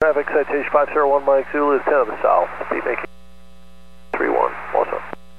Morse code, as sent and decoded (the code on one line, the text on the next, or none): none